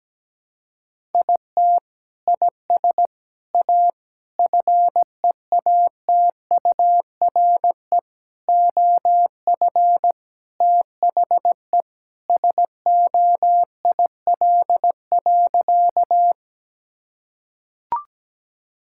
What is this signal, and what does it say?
Morse code, heard as a keyed tone, about 17 wpm, 700 Hz. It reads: IT IS A FEATURE OF THE SOIL.
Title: Sentences from Top 1000 Words - Review Part 5 - 17wpm